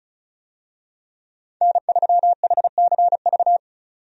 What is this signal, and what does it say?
Morse code, heard as a keyed tone, about 36 wpm, 700 Hz. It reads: N3HCV
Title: USA Call Signs - 5 Characters - 35wpm